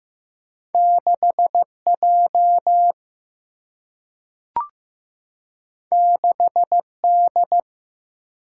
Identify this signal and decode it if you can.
Morse code, heard as a keyed tone, about 15 wpm, 700 Hz. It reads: 6J 6D
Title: Single-Letter Single-Number 15wpm